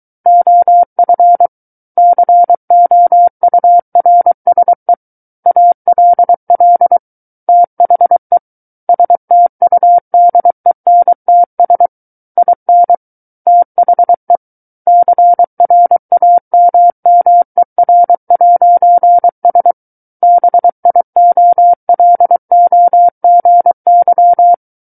Morse code, heard as a keyed tone, about 23 wpm, 700 Hz. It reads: OF COURSE ALL THE STUDENTS IN THE CRAMMER'S BIOLOGY
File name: Book One / 14-Chapter 14 - 23 WPM